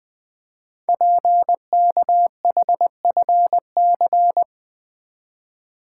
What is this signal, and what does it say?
Morse code, heard as a keyed tone, about 20 wpm, 700 Hz. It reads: PKHFC
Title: Code Groups - 5 Letters - 20wpm